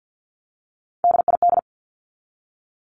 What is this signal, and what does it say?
Morse code, heard as a keyed tone, about 48 wpm, 700 Hz. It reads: DID
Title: Top 200 Words 50wpm